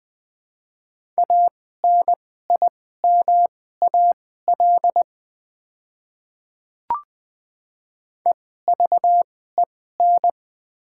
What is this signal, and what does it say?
Morse code, heard as a keyed tone, about 20 wpm, spaced as for 14 wpm, 700 Hz. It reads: ANIMAL EVEN
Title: 2x Character Spacing - Top 500 Words - 20wpm